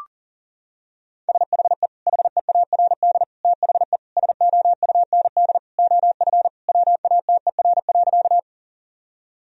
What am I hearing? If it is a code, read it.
Morse code, heard as a keyed tone, about 40 wpm, 700 Hz. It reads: SHE HEARD THE SOUND OF WATER.
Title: Sentences from Top 1000 Words - Review Part 1 - 40wpm